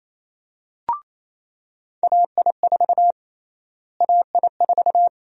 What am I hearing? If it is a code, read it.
Morse code, heard as a keyed tone, about 28 wpm, 700 Hz. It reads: AI4 AI4
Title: USA Call Sign Prefixes - Repeated 2x - 28wpm